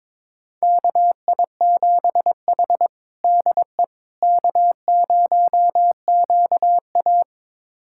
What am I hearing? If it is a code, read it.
Morse code, heard as a keyed tone, about 22 wpm, 700 Hz. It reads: KI7H DE K0QA
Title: USA Callsign DE Callsign - 4 Characters - 22wpm